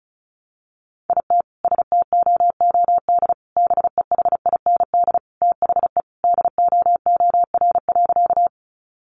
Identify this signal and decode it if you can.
Morse code, heard as a keyed tone, about 34 wpm, 700 Hz. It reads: IT STOOD BEHIND THE DOOR.
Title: Sentences from Top 500 Words 35wpm